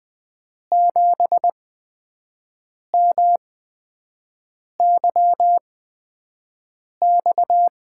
Morse code, heard as a keyed tone, about 20 wpm, spaced as for 5 wpm, 700 Hz. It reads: 7MYX
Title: USA Call Signs - 8x Character Spacing - 20wpm